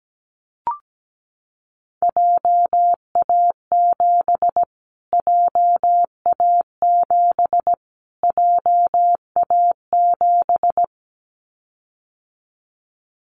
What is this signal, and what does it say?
Morse code, heard as a keyed tone, about 17 wpm, 700 Hz. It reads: JA7 JA7 JA7
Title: Callsign Prefixes for Top 25 DXCC - Repeated 3x - 17wpm